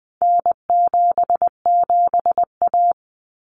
Morse code, heard as a keyed tone, about 20 wpm, 700 Hz. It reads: N77A
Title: Common Cut Numbers - 4-Digit 20wpm